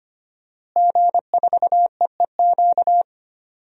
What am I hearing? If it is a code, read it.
Morse code, heard as a keyed tone, about 25 wpm, 700 Hz. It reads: G4EEQ
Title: DXCC-100 Call Signs - 5 Characters - Repeated 2x - 25wpm